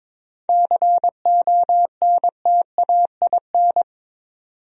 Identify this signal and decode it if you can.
Morse code, heard as a keyed tone, about 26 wpm, 700 Hz. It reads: CONTAIN
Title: Speed-Racing Top 500 Words 22wpm